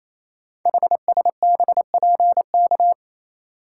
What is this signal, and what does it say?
Morse code, heard as a keyed tone, about 27 wpm, 700 Hz. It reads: HSBPK
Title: Code Groups - 5 Letters - 28wpm